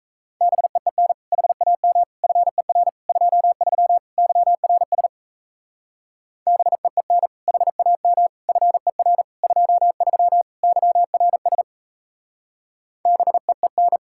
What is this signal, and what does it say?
Morse code, heard as a keyed tone, about 38 wpm, 700 Hz. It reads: BEEN HAM FER 23 YRS BEEN HAM FER 23 YRS BEEN